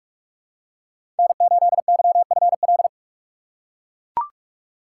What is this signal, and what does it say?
Morse code, heard as a keyed tone, about 43 wpm, 700 Hz. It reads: N8YFL
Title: USA Call Signs - 5 Characters - 4x Character Spacing - 45wpm